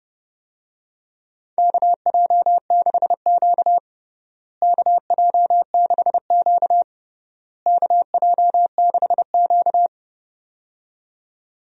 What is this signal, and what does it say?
Morse code, heard as a keyed tone, about 30 wpm, 700 Hz. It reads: KJ6Q KJ6Q KJ6Q
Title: USA Call Signs - 4 Characters - Repeated 3x - 30wpm